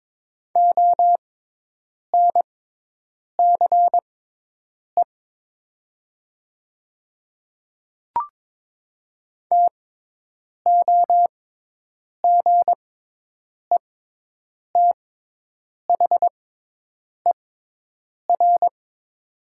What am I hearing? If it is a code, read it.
Morse code, heard as a keyed tone, about 22 wpm, spaced as for 8 wpm, 700 Hz. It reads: ONCE TOGETHER
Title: 6x Character Spacing - Top 500 Words - 22wpm